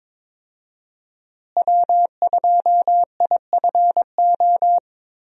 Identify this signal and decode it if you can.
Morse code, heard as a keyed tone, about 22 wpm, 700 Hz. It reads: W2IFO
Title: USA Call Signs - 5 Characters - Repeated 2x - 22wpm